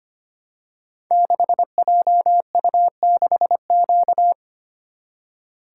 Morse code, heard as a keyed tone, about 25 wpm, 700 Hz. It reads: 6JU6Q
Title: Code Groups - 5 Letter:Numbers - 25wpm